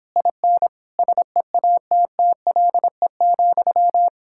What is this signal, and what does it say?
Morse code, heard as a keyed tone, about 29 wpm, 700 Hz. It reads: IN SEATTLE,